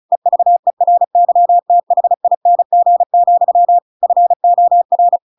Morse code, heard as a keyed tone, about 34 wpm, 700 Hz. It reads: EVERYTHING, FOR
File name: Pinocchio / Pinocchio ch7 35 35 ik1hge